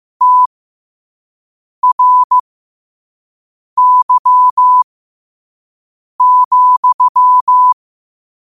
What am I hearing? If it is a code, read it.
Morse code, heard as a keyed tone, about 15 wpm, spaced as for 5 wpm, 1000 Hz. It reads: TRY,
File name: qotdfiles / 5wpm